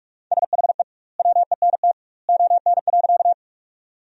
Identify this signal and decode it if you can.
Morse code, heard as a keyed tone, about 42 wpm, 700 Hz. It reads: SHE WENT ON.
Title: 2x Word-Spacing - Sentences from Top 200 Words - 45wpm